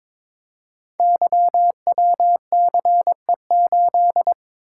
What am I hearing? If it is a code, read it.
Morse code, heard as a keyed tone, about 22 wpm, 700 Hz. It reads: YWCE8
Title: Code Groups - 5 Letter:Numbers - 22wpm